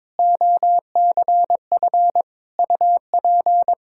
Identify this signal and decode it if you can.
Morse code, heard as a keyed tone, about 22 wpm, 700 Hz. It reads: OCF UP
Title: Practice QSO - Exchange 2 - Info - 22wpm